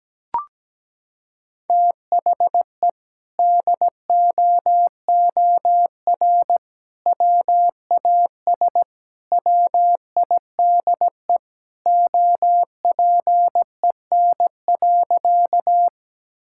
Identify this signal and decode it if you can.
Morse code, heard as a keyed tone, about 17 wpm, 700 Hz. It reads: THE DOOR WAS WIDE OPEN.
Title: Sentences from Top 1000 Words - Review Part 6 - 17wpm